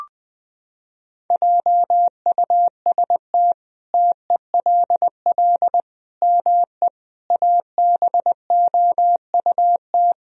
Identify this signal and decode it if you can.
Morse code, heard as a keyed tone, about 21 wpm, 700 Hz. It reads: JUST TELL ME ABOUT
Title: Sentences from Top 1000 Words - Review Part 1 - 20wpm